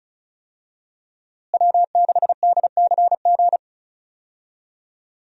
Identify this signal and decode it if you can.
Morse code, heard as a keyed tone, about 36 wpm, 700 Hz. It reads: W6DCG
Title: USA Call Signs - 5 Characters - Repeated 2x - 35wpm